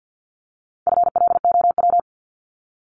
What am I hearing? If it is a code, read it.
Morse code, heard as a keyed tone, about 50 wpm, 700 Hz. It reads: FLOW
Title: Top 1001-1250 Words 50wpm